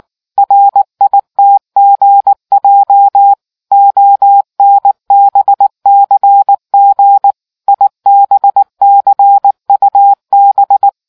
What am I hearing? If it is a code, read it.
Morse code, heard as a keyed tone, about 19 wpm, 800 Hz. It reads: RITGJ ONBCG IBCUB